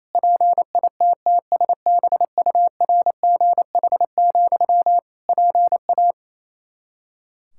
Morse code, heard as a keyed tone, about 28 wpm, 700 Hz. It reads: PITTSBURGH, PA
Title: Practice QSO - Exchange 1 - Beginning - 28wpm